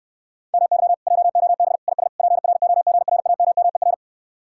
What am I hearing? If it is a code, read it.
Morse code, heard as a keyed tone, about 71 wpm, 700 Hz. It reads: MY JOB IS PROGRAMMER